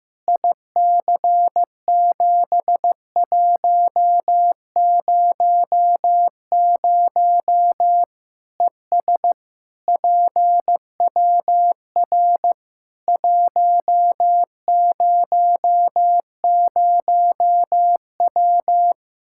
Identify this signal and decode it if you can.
Morse code, heard as a keyed tone, about 16 wpm, 700 Hz. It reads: IC7100 ES PWR 100W
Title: K1USN SST QSO Practice 15wpm